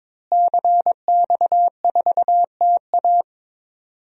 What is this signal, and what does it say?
Morse code, heard as a keyed tone, about 21 wpm, 700 Hz. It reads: CX4TA
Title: DXCC-100 Call Signs - 5 Characters - Repeated 3x - 22wpm